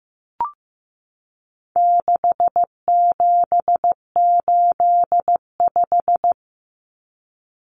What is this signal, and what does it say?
Morse code, heard as a keyed tone, about 15 wpm, 700 Hz. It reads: 6785
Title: Common Cut Numbers - 4-Digit 15wpm